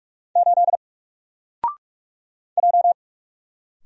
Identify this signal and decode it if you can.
Morse code, heard as a keyed tone, about 46 wpm, 700 Hz. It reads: 8 J
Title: Single Letter-Number 45wpm